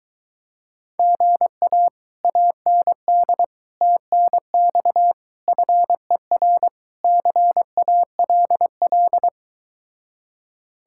Morse code, heard as a keyed tone, about 23 wpm, 700 Hz. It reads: GA AND TNX FER CALL